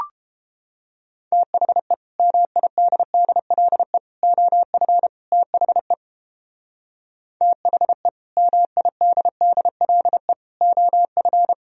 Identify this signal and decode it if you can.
Morse code, heard as a keyed tone, about 31 wpm, 700 Hz. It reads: THE MIDDLE OF THE THE MIDDLE OF